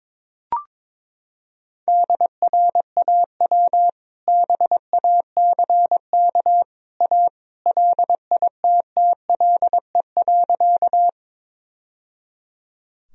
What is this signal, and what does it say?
Morse code, heard as a keyed tone, about 22 wpm, 700 Hz. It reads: DRAW BACK A LITTLE.
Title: Sentences from Top 1000 Words - Review Part 1 - 22wpm